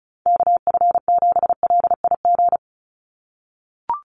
Morse code, heard as a keyed tone, about 34 wpm, 700 Hz. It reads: KF7LIG
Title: USA Call Signs 35wpm